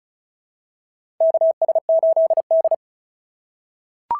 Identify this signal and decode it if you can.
Morse code, heard as a keyed tone, about 36 wpm, 690 Hz. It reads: KS8D
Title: SOTA QSO Practice 35wpm